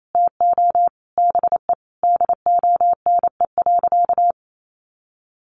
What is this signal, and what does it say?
Morse code, heard as a keyed tone, about 28 wpm, 700 Hz. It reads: TO BE DONE.
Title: Sentences from Top 500 Words 28wpm